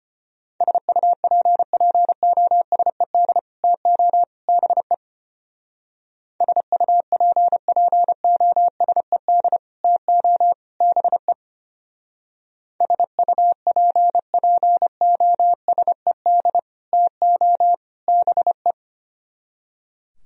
Morse code, heard as a keyed tone, about 29 wpm, 700 Hz. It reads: SUPPOSED TO BE SUPPOSED TO BE SUPPOSED TO BE
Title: Speed-Racing - Sets of 3 Words 25wpm